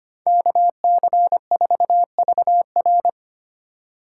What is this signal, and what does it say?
Morse code, heard as a keyed tone, about 25 wpm, 700 Hz. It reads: KC4VR
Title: USA Call Signs - 5 Characters - 6x Character Spacing - 25wpm